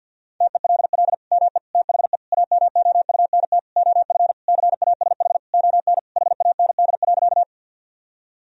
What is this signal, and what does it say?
Morse code, heard as a keyed tone, about 50 wpm, 700 Hz. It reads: TELL ME THE AMOUNT OF CASH ON HAND.
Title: Sentences from Top 1000 Words - Review Part 4 - 50wpm